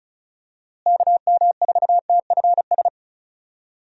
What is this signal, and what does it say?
Morse code, heard as a keyed tone, about 34 wpm, 700 Hz. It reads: KM4TFS
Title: USA Call Signs - 6 Characters - Repeated 3x - 35wpm